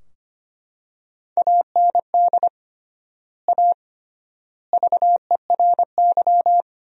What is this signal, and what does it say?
Morse code, heard as a keyed tone, about 25 wpm, 700 Hz. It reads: AND A VERY
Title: 3x Word-Spacing - Sentences from Top 200 Words - 25wpm